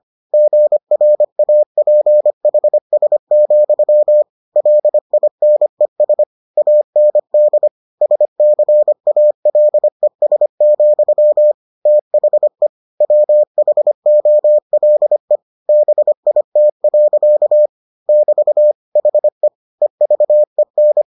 Morse code, heard as a keyed tone, about 25 wpm, 600 Hz. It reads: GRAPHS, LINES AND SCALES, THE WHOLE BIT. <BT> HE EVEN